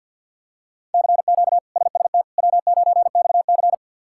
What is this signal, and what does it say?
Morse code, heard as a keyed tone, about 52 wpm, 700 Hz. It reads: CQ SST W9XC
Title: K1USN SST QSO Practice - 50wpm